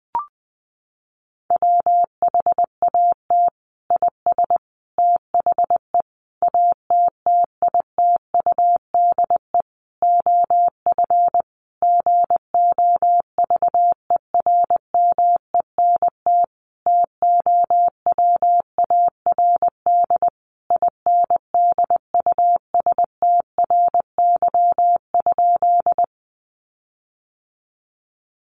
Morse code, heard as a keyed tone, about 20 wpm, 700 Hz. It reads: WHAT IS THE ATTITUDE OF GOVERMENT TOWARD INDUSTRY?
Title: Sentences from Top 1300 Words 20wpm